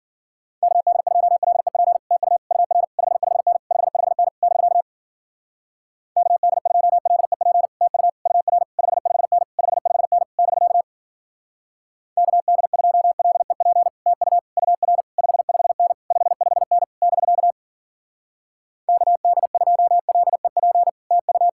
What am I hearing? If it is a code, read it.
Morse code, heard as a keyed tone, about 48 wpm, 700 Hz. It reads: KD2LEP TU UR 55N 55N <BK> KD2LEP TU UR 55N 55N <BK> KD2LEP TU UR 55N 55N <BK> KD2LEP TU
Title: Speed-Racing Common QSO Phrases - 40wpm